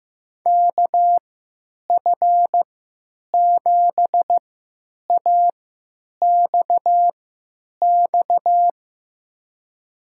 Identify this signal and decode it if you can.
Morse code, heard as a keyed tone, about 15 wpm, spaced as for 9 wpm, 700 Hz. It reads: KF7AXX
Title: USA Call Signs - 6 Characters - 3x Character Spacing - 15wpm